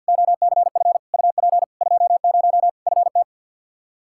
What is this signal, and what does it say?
Morse code, heard as a keyed tone, about 51 wpm, 700 Hz. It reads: OCF UP 20 FT
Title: Practice QSO - Exchange 2 - Info - 50wpm